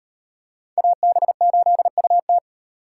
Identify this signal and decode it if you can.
Morse code, heard as a keyed tone, about 34 wpm, 700 Hz. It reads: AB8UT